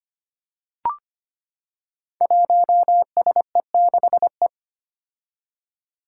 Morse code, heard as a keyed tone, about 25 wpm, 700 Hz. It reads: 1SE6E
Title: Five Challenging Letter-Number Combinations 25wpm